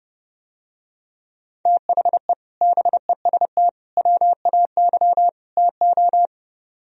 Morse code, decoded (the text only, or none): THE BEST WAY TO